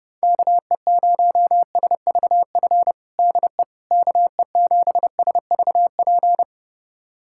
Morse code, heard as a keyed tone, about 30 wpm, 700 Hz. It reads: KE0SVF DE KE7SVP
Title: USA Callsign DE Callsign - 6 Characters - 30wpm